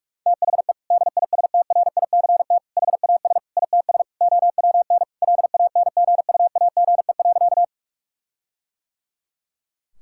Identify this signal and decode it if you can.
Morse code, heard as a keyed tone, about 47 wpm, 700 Hz. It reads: THE DISTRICT HAS ITS OWN LANGUAGE.
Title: Sentences from Top 1600 Words 45wpm